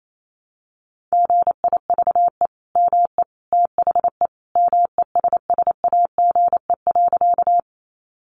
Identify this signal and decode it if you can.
Morse code, heard as a keyed tone, about 28 wpm, 700 Hz. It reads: GIVE ME THE MESSAGE.
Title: Sentences from Top 1100 Words 28wpm